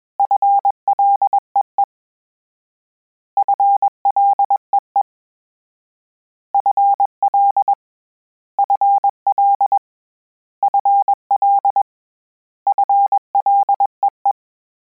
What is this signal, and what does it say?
Morse code, heard as a keyed tone, about 21 wpm, 800 Hz. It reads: FLEE FLEE FL FL FL FLEE
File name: Combinations / Combinations-mp3s / Combinations@20